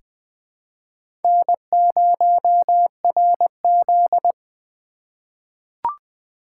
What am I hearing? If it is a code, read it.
Morse code, heard as a keyed tone, about 20 wpm, 700 Hz. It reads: N0RZ
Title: 2023 SOTA Chaser Call Signs - 20wpm